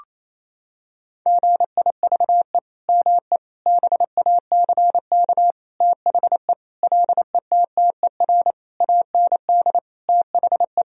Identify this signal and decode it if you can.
Morse code, heard as a keyed tone, about 28 wpm, 700 Hz. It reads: GIVE ME BACK THE LETTER AND THE
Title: Sentences from Top 1000 Words - Review Part 4 - 28wpm